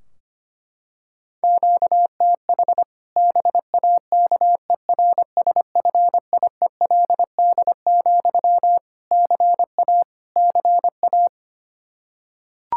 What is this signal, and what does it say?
Morse code, heard as a keyed tone, about 25 wpm, 700 Hz. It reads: QTH BAKERSFIELD, CA CA
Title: Practice QSO - Exchange 1 - Beginning - 25wpm